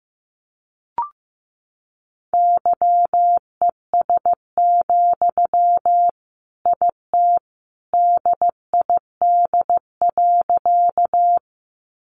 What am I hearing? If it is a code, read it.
Morse code, heard as a keyed tone, about 15 wpm, 700 Hz. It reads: YES, IT DID.
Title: Sentences from Top 500 Words 15wpm